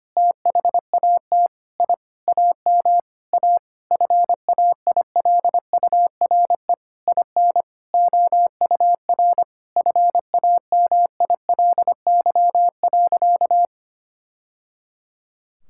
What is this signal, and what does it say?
Morse code, heard as a keyed tone, about 25 wpm, 700 Hz. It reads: THAT I AM A FAILURE IN OUR FAMILY.
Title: Sentences from Top 1000 Words - Review Part 6 - 25wpm